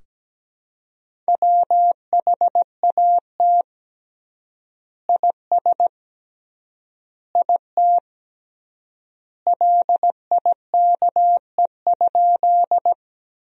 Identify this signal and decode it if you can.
Morse code, heard as a keyed tone, about 17 wpm, 700 Hz. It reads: WHAT IS IT LIKE?
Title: 3x Word-Spacing - Sentences from Top 100 Words - 17wpm